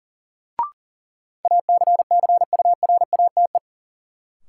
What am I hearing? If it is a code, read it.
Morse code, heard as a keyed tone, about 40 wpm, 700 Hz. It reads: ACCURATE
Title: Sentences from Top 1200 Words - New Word Drill - 40wpm